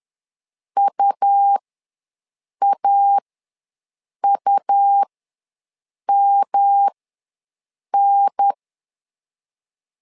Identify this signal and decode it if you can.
Morse code, heard as a keyed tone, about 10 wpm, spaced as for 6 wpm, 800 Hz. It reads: UAUMN
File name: LEZIONI / 3 MWRO / 3 M ws 10wpm 2121